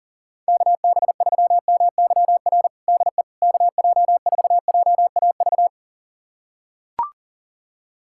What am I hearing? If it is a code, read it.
Morse code, heard as a keyed tone, about 40 wpm, 700 Hz. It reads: KB3MYR DE KJ4JAV
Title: USA Callsign DE Callsign - 6 Characters - 40wpm